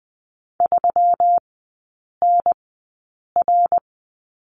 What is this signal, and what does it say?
Morse code, heard as a keyed tone, about 20 wpm, spaced as for 8 wpm, 700 Hz. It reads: 3NR
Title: 152 Lesson 25e - Letter p - Three Characters 20wpm